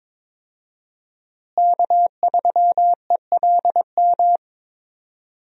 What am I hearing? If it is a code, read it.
Morse code, heard as a keyed tone, about 22 wpm, 700 Hz. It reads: K3ELM
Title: USA Call Signs - 5 Characters - Repeated 2x - 22wpm